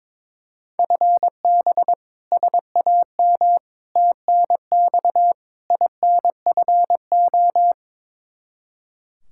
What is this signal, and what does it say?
Morse code, heard as a keyed tone, about 22 wpm, 700 Hz. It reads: FB SAM TNX INFO
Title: Practice QSO - Exchange 3 - More Info - 22wpm